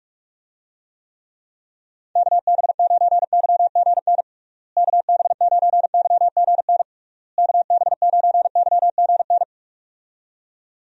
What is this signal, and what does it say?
Morse code, heard as a keyed tone, about 41 wpm, 700 Hz. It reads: KB9YGN KB9YGN KB9YGN
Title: USA Call Signs - 6 Characters - Repeated 3x - 45wpm